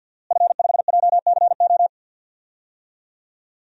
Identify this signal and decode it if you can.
Morse code, heard as a keyed tone, about 50 wpm, 700 Hz. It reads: F5JCO
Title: DXCC-100 Call Signs - 5 Characters - Repeated 3x - 50wpm